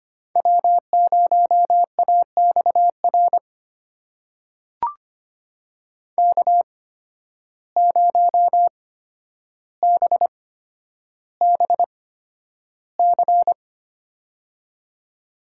Morse code, heard as a keyed tone, about 25 wpm, spaced as for 11 wpm, 700 Hz. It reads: W0AXR K0BBC
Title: USA Call Signs - 5 Characters - 8x Character Spacing - 25wpm